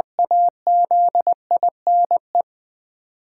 Morse code, heard as a keyed tone, about 20 wpm, 700 Hz. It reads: AZINE
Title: 8 Letter Words - First 3 Letters Given - 20wpm